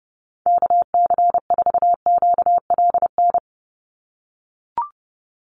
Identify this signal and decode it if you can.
Morse code, heard as a keyed tone, about 30 wpm, 700 Hz. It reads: KC4QLN
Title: USA Call Signs 30wpm